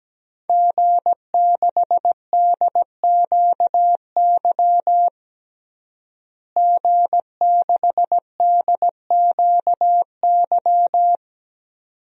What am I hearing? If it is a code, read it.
Morse code, heard as a keyed tone, about 17 wpm, 700 Hz. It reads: G6DQY G6DQY
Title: DXCC-100 Call Signs - 5 Characters - Repeated 2x - 17wpm